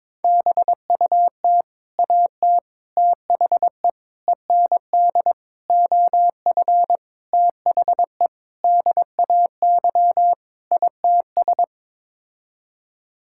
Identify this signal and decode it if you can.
Morse code, heard as a keyed tone, about 22 wpm, 700 Hz. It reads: BUT AT THE END OF THE DAY ITS
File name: practice / 22wpm / Sets of 8 Words - 22wpm